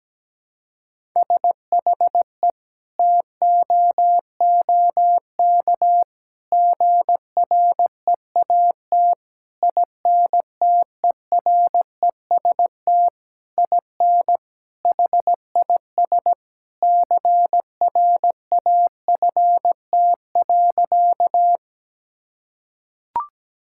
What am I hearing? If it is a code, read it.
Morse code, heard as a keyed tone, about 17 wpm, 700 Hz. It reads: SHE TOOK GREAT INTEREST IN HIS CRAFT.
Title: Sentences from Top 1000 Words - Review Part 6 - 17wpm